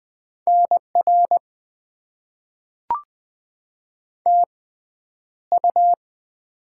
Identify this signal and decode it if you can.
Morse code, heard as a keyed tone, about 20 wpm, spaced as for 10 wpm, 700 Hz. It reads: NR TU
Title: Top 90 QSO Elements - 6x Character Spacing - 20wpm